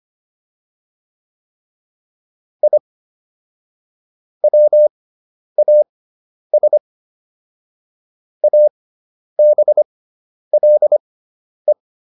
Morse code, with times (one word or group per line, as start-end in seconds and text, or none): 2.6-2.8 I
4.4-6.8 WAS
8.4-11.7 ABLE